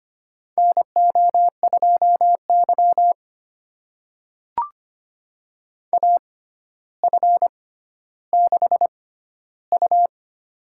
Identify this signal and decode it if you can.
Morse code, heard as a keyed tone, about 25 wpm, spaced as for 13 wpm, 700 Hz. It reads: NO2Y AF6U
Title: USA Call Signs - 4 Characters - 6x Character Spacing - 25wpm